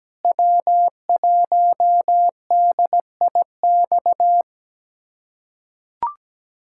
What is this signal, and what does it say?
Morse code, heard as a keyed tone, about 17 wpm, 700 Hz. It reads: W1DIX
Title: USA Callsign DE Callsign 17wpm